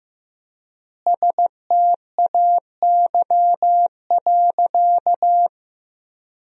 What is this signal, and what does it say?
Morse code, heard as a keyed tone, about 16 wpm, 700 Hz. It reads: STAY.
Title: 4x Word-Spacing - Sentences from Top 500 Words - 15wpm